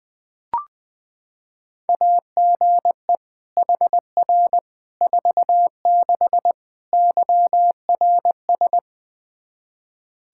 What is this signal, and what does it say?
Morse code, heard as a keyed tone, about 21 wpm, 700 Hz. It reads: AGE HR 46 YRS
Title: Practice QSO - Exchange 3 - More Info - 20wpm